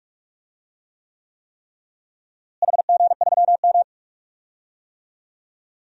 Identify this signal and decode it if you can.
Morse code, heard as a keyed tone, about 45 wpm, 700 Hz. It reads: HG3M
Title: DXCC-100 Call Signs - 4 Characters - Repeated 3x - 45wpm